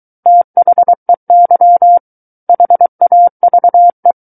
Morse code, heard as a keyed tone, about 23 wpm, 700 Hz. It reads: THEY HAVE
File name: Book One / 14-Chapter 14 - 23 WPM